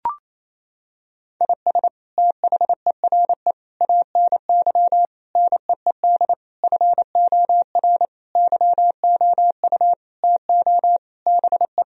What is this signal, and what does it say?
Morse code, heard as a keyed tone, about 28 wpm, 700 Hz. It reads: IS THERE ANY NEED FOR YOU TO BE
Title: Sentences from Top 1000 Words - Review Part 4 - 28wpm